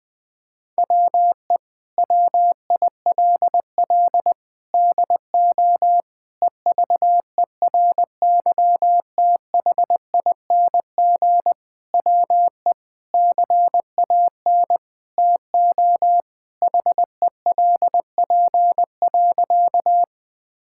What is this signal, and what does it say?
Morse code, heard as a keyed tone, about 20 wpm, 700 Hz. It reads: WE WILL DO EVERYTHING WE CAN TO HELP.
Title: Sentences with 6-Word Phrases - 20wpm